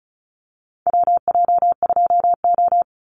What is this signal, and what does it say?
Morse code, heard as a keyed tone, about 36 wpm, 700 Hz. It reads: WJ2O
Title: Call Signs - 35wpm